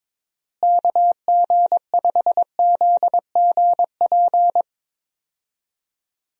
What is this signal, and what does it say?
Morse code, heard as a keyed tone, about 22 wpm, 700 Hz. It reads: KG5ZGP
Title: USA Call Signs - 6 Characters - Repeated 3x - 22wpm